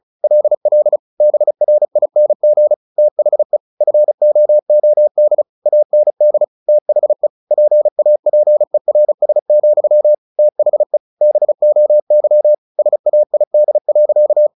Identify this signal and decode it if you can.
Morse code, heard as a keyed tone, about 34 wpm, 600 Hz. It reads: LL BRING THE FOOD AND THE PAPERS, THE BOY SAID.
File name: the old man and the sea 2 35wpm